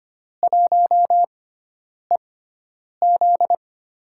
Morse code, heard as a keyed tone, about 25 wpm, spaced as for 9 wpm, 700 Hz. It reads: 1EZ
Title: USA Call Signs - 4 Characters - 6x Character Spacing - 25wpm